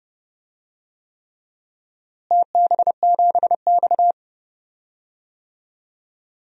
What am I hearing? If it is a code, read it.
Morse code, heard as a keyed tone, about 31 wpm, 700 Hz. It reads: TB7X
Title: DXCC-100 Call Signs - 4 Characters - Repeated 3x - 30wpm